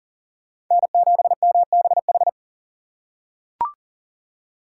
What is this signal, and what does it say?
Morse code, heard as a keyed tone, about 40 wpm, 700 Hz. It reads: N7MBH
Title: USA Call Signs - 6x Character Spacing - 40wpm